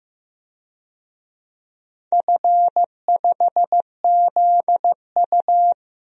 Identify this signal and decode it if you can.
Morse code, heard as a keyed tone, about 15 wpm, 700 Hz. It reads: F5ZU
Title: DXCC-100 Call Signs - 4 Characters - Repeated 3x - 15wpm